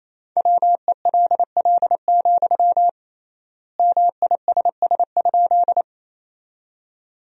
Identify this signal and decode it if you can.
Morse code, heard as a keyed tone, about 27 wpm, 700 Hz. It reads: WELL, MISS?
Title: 3x Word-Spacing - Sentences from Top 500 Words - 28wpm